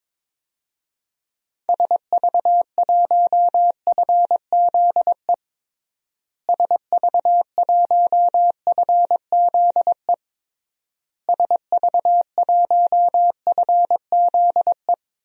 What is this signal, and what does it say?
Morse code, heard as a keyed tone, about 22 wpm, 700 Hz. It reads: SV1FZE SV1FZE SV1FZE
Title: DXCC-100 Call Signs - 6 Characters - Repeated 3x - 22wpm